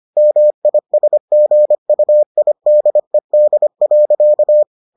Morse code, heard as a keyed tone, about 25 wpm, 600 Hz. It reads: MISGUIDED.